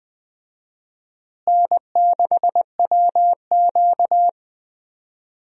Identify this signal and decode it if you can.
Morse code, heard as a keyed tone, about 20 wpm, 700 Hz. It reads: N6WQ
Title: USA Call Signs - 4 Characters - Repeated 3x - 20wpm